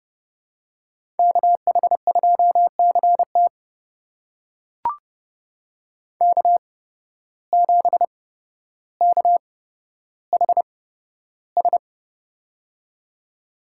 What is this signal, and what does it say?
Morse code, heard as a keyed tone, about 29 wpm, spaced as for 13 wpm, 700 Hz. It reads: KH2CT K7KHS
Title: USA Call Signs - 5 Characters - 8x Character Spacing - 30wpm